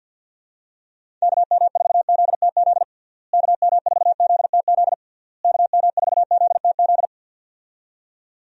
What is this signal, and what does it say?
Morse code, heard as a keyed tone, about 51 wpm, 700 Hz. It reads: KM4ZTZ KM4ZTZ KM4ZTZ
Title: USA Call Signs - 6 Characters - Repeated 3x - 50wpm